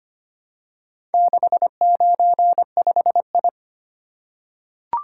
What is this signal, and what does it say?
Morse code, heard as a keyed tone, about 25 wpm, 700 Hz. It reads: 695I
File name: practice / 25wpm / Four Letter-Number Combinations - 25wpm